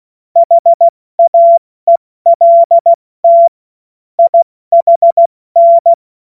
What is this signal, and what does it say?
Morse code, heard as a keyed tone, about 16 wpm, 680 Hz. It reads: HAELT IHN
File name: Morsekurs / Witze / Witz 18. Januar 2022, 70 F80